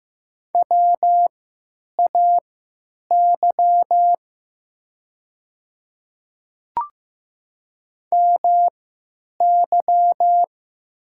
Morse code, heard as a keyed tone, about 15 wpm, spaced as for 9 wpm, 700 Hz. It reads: WAY MY
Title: 3x Character Spacing - Top 100 Words - 15wpm